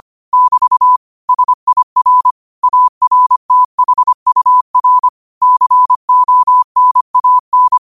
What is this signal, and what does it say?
Morse code, heard as a keyed tone, about 25 wpm, 1000 Hz. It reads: <BT> SIR ARTHUR CONAN
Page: Quotes for Wed, 13 Aug 2025 in Morse Code at 25 words per minute.